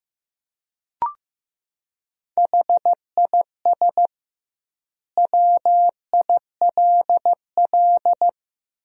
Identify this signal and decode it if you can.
Morse code, heard as a keyed tone, about 15 wpm, 700 Hz. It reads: HIS WILL
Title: 2x Word-Spacing - Sentences from Top 500 Words - 15wpm